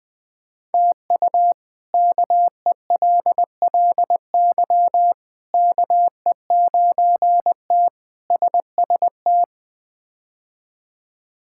Morse code, heard as a keyed tone, about 21 wpm, 700 Hz. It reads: TU KELLY KE9T SST
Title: K1USN SST QSO Practice - 20wpm